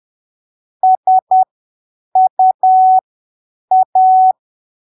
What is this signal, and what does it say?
Morse code, heard as a keyed tone, about 10 wpm, spaced as for 7 wpm, 750 Hz. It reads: SUA
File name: arquivos / texto / texto 28 10ppm